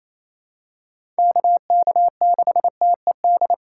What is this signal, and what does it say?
Morse code, heard as a keyed tone, about 28 wpm, 700 Hz. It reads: KK6TED
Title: USA Call Signs - 6 Characters - 4x Character Spacing - 28wpm